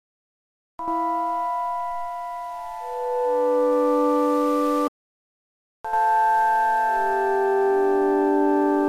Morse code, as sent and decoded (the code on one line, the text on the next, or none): none